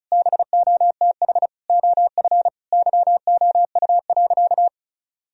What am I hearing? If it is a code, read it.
Morse code, heard as a keyed tone, about 36 wpm, 700 Hz. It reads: BOTH OF YOU.
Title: Sentences from Top 1000 Words - Review Part 1 - 35wpm